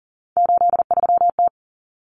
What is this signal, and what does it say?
Morse code, heard as a keyed tone, about 40 wpm, 700 Hz. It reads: 83T
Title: Common Cut Numbers - 3-Digit 40wpm